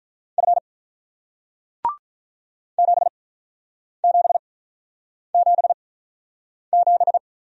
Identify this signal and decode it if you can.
Morse code, heard as a keyed tone, about 47 wpm, spaced as for 9 wpm, 700 Hz. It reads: F 7777